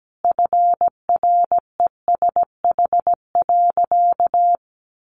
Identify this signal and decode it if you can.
Morse code, heard as a keyed tone, about 17 wpm, 700 Hz. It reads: FRESH.
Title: Sentences from Top 1200 Words 17wpm